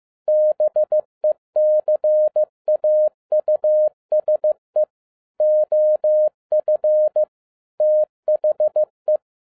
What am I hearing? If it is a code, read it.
Morse code, heard as a keyed tone, about 15 wpm, 610 Hz. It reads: BECAUSE OF THE
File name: 2011-10-mp3 / n02-FIELD DAY CONTEST QRV FROM SAKAE VILLAGE 15wpm